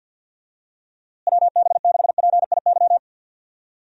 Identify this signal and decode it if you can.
Morse code, heard as a keyed tone, about 51 wpm, 700 Hz. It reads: WB6PIY
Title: USA Call Signs - 6 Characters - Repeated 2x - 50wpm